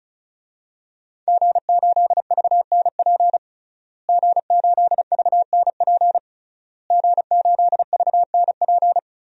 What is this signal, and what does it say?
Morse code, heard as a keyed tone, about 36 wpm, 700 Hz. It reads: G8VNP G8VNP G8VNP
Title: DXCC-100 Call Signs - 5 Characters - Repeated 3x - 35wpm